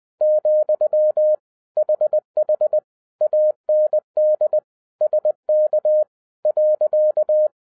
Morse code, heard as a keyed tone, about 20 wpm, 610 Hz. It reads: , HH AND SK .